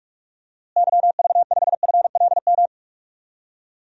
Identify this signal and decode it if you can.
Morse code, heard as a keyed tone, about 46 wpm, 700 Hz. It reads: YV5FLM
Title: DXCC-100 Call Signs - 6 Characters - Repeated 3x - 45wpm